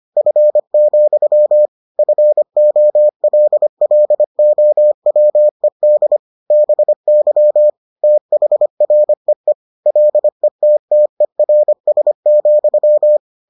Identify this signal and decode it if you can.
Morse code, heard as a keyed tone, about 25 wpm, 600 Hz. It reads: F, FOLLOWED BY THREE LETTERS,